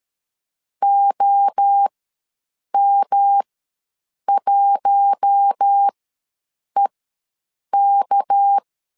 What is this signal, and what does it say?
Morse code, heard as a keyed tone, about 13 wpm, spaced as for 7 wpm, 800 Hz. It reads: OM1EK
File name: LEZIONI / 7 GF9Z / 7 GF9 ws 12wpm 3344